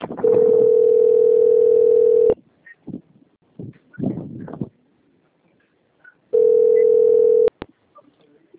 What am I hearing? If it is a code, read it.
Ringback tone (North America), 440 + 480 Hz.